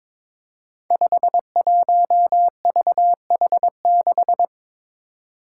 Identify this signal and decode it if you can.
Morse code, heard as a keyed tone, about 23 wpm, 700 Hz. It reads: 51VH6
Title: Five Challenging Letter-Number Combinations 22wpm